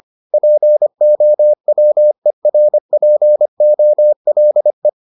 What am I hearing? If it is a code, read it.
Morse code, heard as a keyed tone, about 25 wpm, 600 Hz. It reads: POWERPOLE